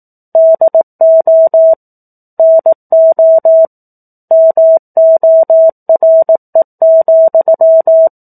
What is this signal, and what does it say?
Morse code, heard as a keyed tone, about 18 wpm, 650 Hz. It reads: DO NO MORE,
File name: 04-Chapter-04 18 WPM